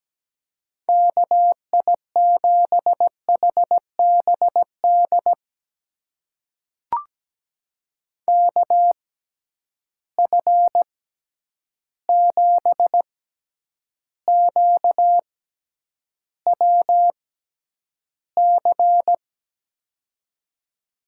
Morse code, heard as a keyed tone, about 17 wpm, spaced as for 9 wpm, 700 Hz. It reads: KI7HBD KF7QWC
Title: USA Call Signs - 6 Characters - 6x Character Spacing - 17wpm